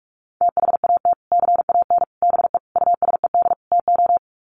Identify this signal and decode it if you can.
Morse code, heard as a keyed tone, about 44 wpm, 700 Hz. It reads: THAT CAN BE USED TO
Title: Sets of 5 Words Encore 45wpm